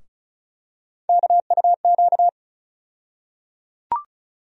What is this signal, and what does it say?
Morse code, heard as a keyed tone, about 36 wpm, 700 Hz. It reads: KUQ
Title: USA Call Sign Postfixes - 3 Letters 35wpm